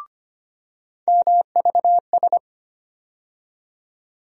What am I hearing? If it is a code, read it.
Morse code, heard as a keyed tone, about 25 wpm, 700 Hz. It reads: MVS
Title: USA Call Sign Postfixes - 3 Letters 25wpm